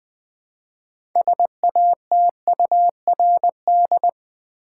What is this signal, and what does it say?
Morse code, heard as a keyed tone, about 20 wpm, 700 Hz. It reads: SATURD
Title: Build A Word - 3 Letters At A Time - 8-Letter Words - 20wpm